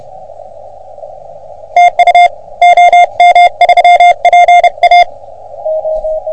Morse code, heard as a keyed tone, about 30 wpm, 720 Hz. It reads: TU OM3PA
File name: OM3PA PILUP CUC 388